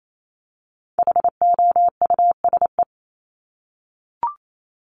Morse code, heard as a keyed tone, about 28 wpm, 700 Hz. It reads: HOUSE
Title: Top 200 Words Repeated 28wpm